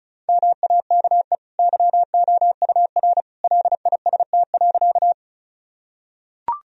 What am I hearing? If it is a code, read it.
Morse code, heard as a keyed tone, about 34 wpm, 700 Hz. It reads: MAKE YOUR LIST.
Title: Sentences from Top 1000 Words - Review Part 2 - 35wpm